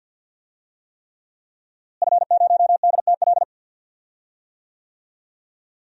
Morse code, heard as a keyed tone, about 46 wpm, 700 Hz. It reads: F0DTL
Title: DXCC-100 Call Signs - 5 Characters - Repeated 3x - 50wpm